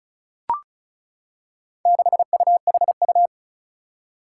Morse code, heard as a keyed tone, about 35 wpm, 700 Hz. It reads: 6UHU
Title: Four Challenging Letter-Number Combinations 35wpm